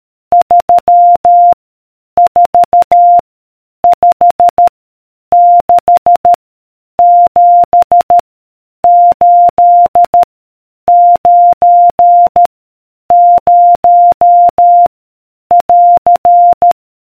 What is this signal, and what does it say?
Morse code, heard as a keyed tone, about 13 wpm, 700 Hz. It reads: 3 4 5 6 7 8 9 0 <AR>